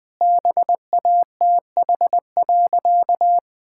Morse code, heard as a keyed tone, about 21 wpm, 700 Hz. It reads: BATH.
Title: Sentences from Top 1600 Words 20wpm